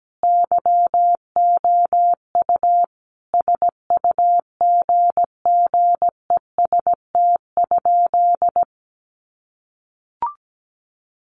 Morse code, heard as a keyed tone, about 17 wpm, 700 Hz. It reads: YOU SUGGEST?
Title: Sentences from Top 1300 Words 17wpm